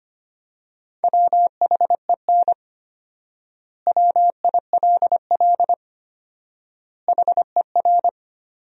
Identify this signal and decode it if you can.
Morse code, heard as a keyed tone, about 25 wpm, 700 Hz. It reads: WHEN WILL HER
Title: 4x Word-Spacing - Sentences from Top 200 Words - 25wpm